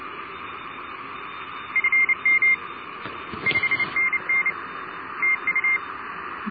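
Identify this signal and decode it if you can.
Morse code, heard as a keyed tone, about 29 wpm, 2100 Hz, with noise. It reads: FM 5NN TU
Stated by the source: QRP pile-up on 14060 kHz, worked with home-built equipment.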